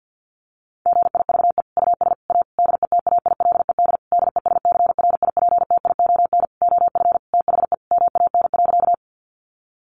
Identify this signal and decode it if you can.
Morse code, heard as a keyed tone, about 50 wpm, 700 Hz. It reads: GIVE US A DETAILED DESCRIPTION OF THE MAN.
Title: Sentences from Top 1500 Words 50wpm